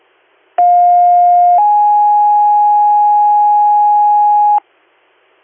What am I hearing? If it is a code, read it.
Two-tone page, heard as a one-second tone, then a three-second tone, about 710 then 830 Hz. It is heard from 0.6 to 4.6 s.